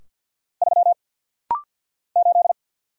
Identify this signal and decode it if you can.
Morse code, heard as a keyed tone, about 50 wpm, 700 Hz. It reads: 3 8
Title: Single Letter-Number Alt - Rapid-Fire 50wpm